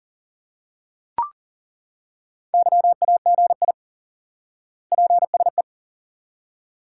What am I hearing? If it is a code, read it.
Morse code, heard as a keyed tone, about 40 wpm, 700 Hz. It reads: YAGI PSE
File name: IQE Course 3 - Lesson 89 - YAGI - Two Elements - 40wpm